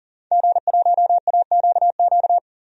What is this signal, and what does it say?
Morse code, heard as a keyed tone, about 39 wpm, 700 Hz. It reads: G1AQQ
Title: DXCC-100 Call Signs - 5 Characters - 40wpm